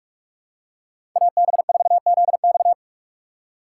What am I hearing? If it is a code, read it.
Morse code, heard as a keyed tone, about 48 wpm, 700 Hz. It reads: AB4ZX
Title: USA Call Signs - 5 Characters - 6x Character Spacing - 45wpm